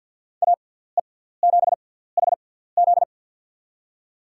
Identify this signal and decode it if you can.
Morse code, heard as a keyed tone, about 47 wpm, spaced as for 17 wpm, 700 Hz. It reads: AE7HZ
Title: USA Call Signs - 6x Character Spacing - 50wpm